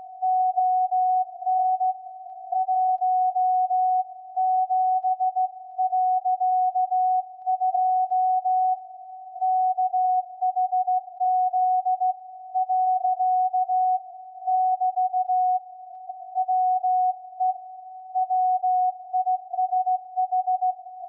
Morse code, heard as a keyed tone, about 16 wpm, 740 Hz. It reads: ON 17.2 KHZ . <BT> WE WISH
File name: iz7slz / VLF / SAQ 24-12-2021